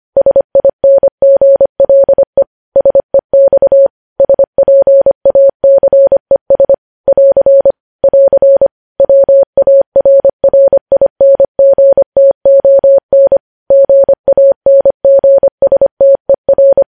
Morse code, heard as a keyed tone, about 25 wpm, 550 Hz. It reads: SINGLE SEX SPACES <AR> <AR> WARRINGTON GANGSTER
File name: podcast / News headlines-1x-25x25